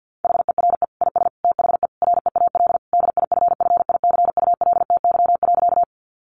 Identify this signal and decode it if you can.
Morse code, heard as a keyed tone, about 50 wpm, 700 Hz. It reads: HERE IS THE REAL DIFFICULTY.
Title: Sentences from Top 1500 Words 50wpm